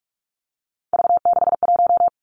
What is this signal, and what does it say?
Morse code, heard as a keyed tone, about 46 wpm, 700 Hz. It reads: V61
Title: Three Challenging Letter-Number Combinations 45wpm